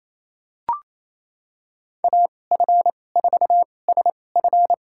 Morse code, heard as a keyed tone, about 28 wpm, spaced as for 20 wpm, 700 Hz. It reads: AF4SF
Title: USA Call Signs - 2x Character Spacing - 28wpm